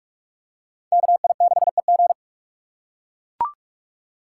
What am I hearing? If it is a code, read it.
Morse code, heard as a keyed tone, about 42 wpm, 700 Hz. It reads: KI6EG
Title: USA Call Signs - 5 Characters - 4x Character Spacing - 45wpm